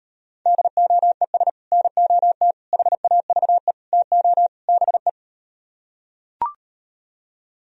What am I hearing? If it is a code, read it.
Morse code, heard as a keyed tone, about 38 wpm, 700 Hz. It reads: DOES NOT HAVE TO BE